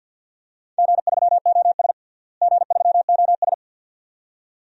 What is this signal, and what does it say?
Morse code, heard as a keyed tone, about 53 wpm, 700 Hz. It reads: G3OS G3OS
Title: DXCC-100 Call Signs - 4 Characters - Repeated 2x - 50wpm